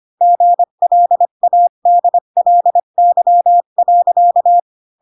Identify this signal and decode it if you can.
Morse code, heard as a keyed tone, about 25 wpm, 700 Hz. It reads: GLADLY.
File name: Pinocchio / Pinocchio ch7 20 25 ik1hge